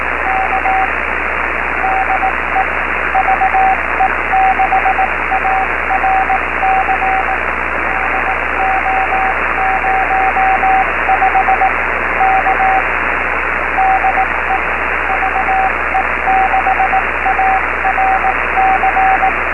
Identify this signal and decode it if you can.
Morse code, heard as a keyed tone, about 18 wpm, 760 Hz, with radio noise. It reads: K DE VE6ARC DO05 K DE VE6ARC
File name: beacons / ve6arc-b